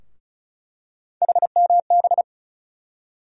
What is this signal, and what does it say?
Morse code, heard as a keyed tone, about 34 wpm, 700 Hz. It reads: HMB
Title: USA Call Sign Postfixes - 3 Letters 35wpm